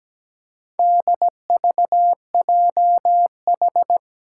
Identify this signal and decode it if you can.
Morse code, heard as a keyed tone, about 17 wpm, 700 Hz. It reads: DVJH
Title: Four Challenging Letter-Number Combinations 17wpm